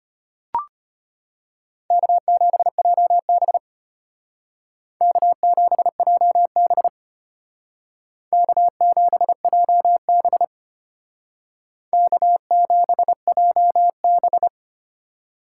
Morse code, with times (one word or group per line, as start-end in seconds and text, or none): 1.9-3.6 K7JB
5.0-6.9 K7JB
8.3-10.4 K7JB
11.9-14.5 K7JB